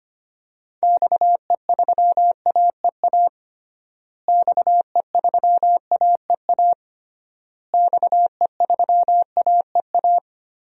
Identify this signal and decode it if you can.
Morse code, heard as a keyed tone, about 25 wpm, 700 Hz. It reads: XE3AEA XE3AEA XE3AEA
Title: DXCC-100 Call Signs - 6 Characters - Repeated 3x - 25wpm